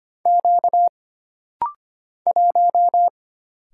Morse code, heard as a keyed tone, about 25 wpm, 700 Hz. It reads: Q 1
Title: Single Letter-Number Alt - Rapid-Fire 25wpm